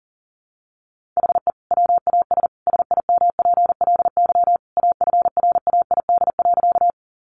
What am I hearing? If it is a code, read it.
Morse code, heard as a keyed tone, about 40 wpm, 700 Hz. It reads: HE WAS SIMPLY AFRAID.
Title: Sentences from Top 1500 Words 40wpm